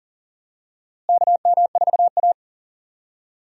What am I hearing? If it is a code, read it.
Morse code, heard as a keyed tone, about 40 wpm, 700 Hz. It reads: KM4A
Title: USA Call Signs - 4 Characters - Repeated 2x - 40wpm